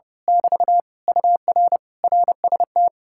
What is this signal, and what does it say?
Morse code, heard as a keyed tone, about 28 wpm, 700 Hz. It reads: <BT> UR RST
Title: Speed-Racing Common QSO Phrases - 30wpm